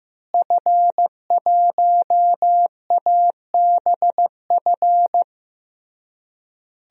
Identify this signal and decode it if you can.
Morse code, heard as a keyed tone, about 15 wpm, 700 Hz. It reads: F1ABF
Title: DXCC-100 Call Signs - 5 Characters - Repeated 3x - 15wpm